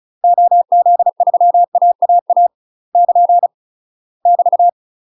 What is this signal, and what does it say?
Morse code, heard as a keyed tone, about 34 wpm, 700 Hz. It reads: OZ3AAA <KN> <BT>
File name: UK MASTER QSOs TRAINING / 35WPM / UK MASTER QSOs 0040